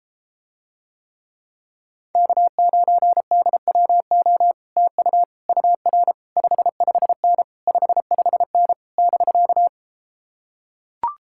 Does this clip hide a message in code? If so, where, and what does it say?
Yes: Morse code from 2.2 to 9.7 s, heard as a keyed tone, about 33 wpm, 700 Hz. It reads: K9DWO TU UR 55N 55N <BK>